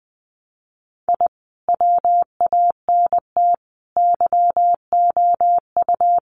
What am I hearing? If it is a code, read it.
Morse code, heard as a keyed tone, about 20 wpm, 700 Hz. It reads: I WANT YOU
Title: Sets of 4 Words 20wpm